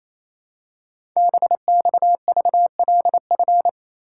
Morse code, heard as a keyed tone, about 29 wpm, 700 Hz. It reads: BXVLF
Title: Code Groups - 5 Letters - 28wpm